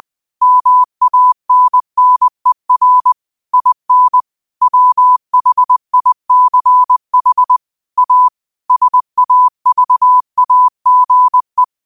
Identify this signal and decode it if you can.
Morse code, heard as a keyed tone, about 20 wpm, 1000 Hz. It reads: MANNER IN WHICH A SAVAGE
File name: qotdfiles / 20wpm